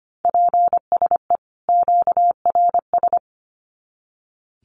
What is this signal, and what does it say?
Morse code, heard as a keyed tone, about 25 wpm, 700 Hz. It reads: PSE QRS
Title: Common QSO Phrases 25wpm